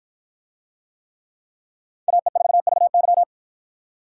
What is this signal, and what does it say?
Morse code, heard as a keyed tone, about 51 wpm, 700 Hz. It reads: AE4VY